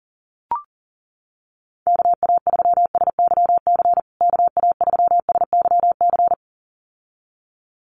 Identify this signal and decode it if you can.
Morse code, heard as a keyed tone, about 40 wpm, 700 Hz. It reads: KA3SYC KA3SYC
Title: USA Call Signs Repeated 2x 40wpm